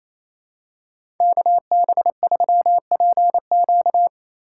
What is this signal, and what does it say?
Morse code, heard as a keyed tone, about 28 wpm, 700 Hz. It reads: KB3PQ
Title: USA Call Signs - 5 Characters - 2x Character Spacing - 28wpm